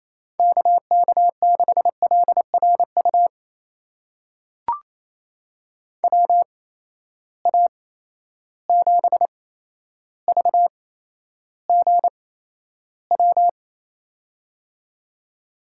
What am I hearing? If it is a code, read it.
Morse code, heard as a keyed tone, about 28 wpm, spaced as for 12 wpm, 700 Hz. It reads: KK6LRU WA7VGW
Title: USA Call Signs - 6 Characters - 8x Character Spacing - 28wpm